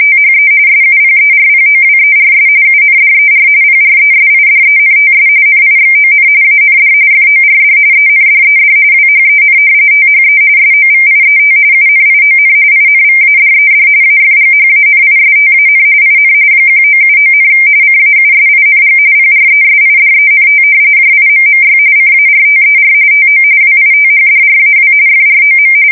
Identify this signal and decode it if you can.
RTTY, about 45.45 baud. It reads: THE SUPERPOWERS UNDER PRESIDENTS NIXON AND FORD ENDED WHEN THE SOVIET UNION INVADED AFGHANISTAN IN 1979, CAUSING PRESIDENT JIMMY CARTER TO WITHDRAW FROM
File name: fsk170 / battleshipnj